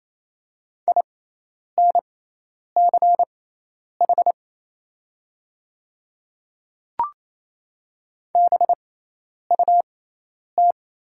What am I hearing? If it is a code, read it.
Morse code, heard as a keyed tone, about 28 wpm, spaced as for 10 wpm, 700 Hz. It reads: INCH BUT
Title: 6x Character Spacing - Top 500 Words - 28wpm